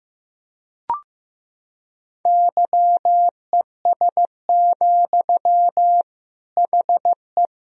Morse code, heard as a keyed tone, about 15 wpm, 700 Hz. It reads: YES, HE
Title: Sentences from Top 1000 Words - Review Part 4 - 15wpm